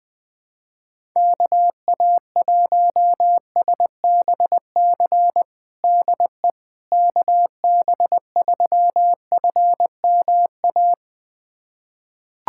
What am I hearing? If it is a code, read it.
Morse code, heard as a keyed tone, about 20 wpm, 700 Hz. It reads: KA1SBC DE KB3FMA
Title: USA Callsign DE Callsign - 6 Characters - 20wpm